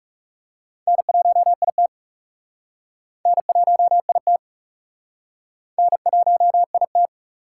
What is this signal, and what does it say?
Morse code, heard as a keyed tone, about 39 wpm, 690 Hz. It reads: N1IT N1IT N1IT